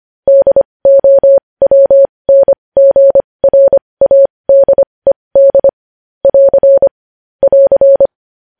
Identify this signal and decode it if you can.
Morse code, heard as a keyed tone, about 25 wpm, spaced as for 20 wpm, 550 Hz. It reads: DOWNGRADED <AR> <AR>
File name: podcast / News headlines-1x-25x20